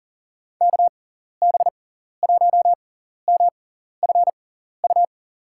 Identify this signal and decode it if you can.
Morse code, heard as a keyed tone, about 40 wpm, spaced as for 14 wpm, 700 Hz. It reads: KB1MFU
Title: USA Call Signs - 6x Character Spacing - 40wpm